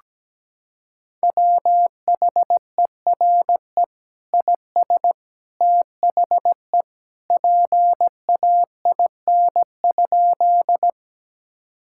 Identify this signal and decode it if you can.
Morse code, heard as a keyed tone, about 17 wpm, 700 Hz. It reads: WHERE IS THE PAIN?
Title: Sentences from Top 1600 Words 17wpm